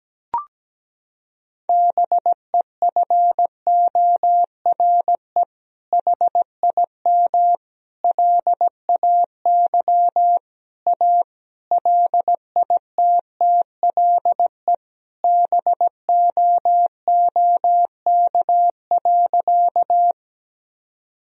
Morse code, heard as a keyed tone, about 17 wpm, 700 Hz. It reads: BEFORE HIM LAY A LITTLE BOOK.
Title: Sentences from Top 1000 Words - Review Part 2 - 17wpm